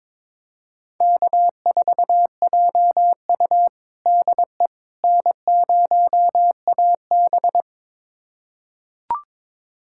Morse code, heard as a keyed tone, about 22 wpm, 700 Hz. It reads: K4JU DE N0AB
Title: USA Callsign DE Callsign - 4 Characters - 22wpm